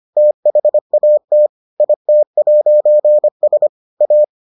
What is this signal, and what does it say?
Morse code, heard as a keyed tone, about 25 wpm, 600 Hz. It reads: THAT IT'S A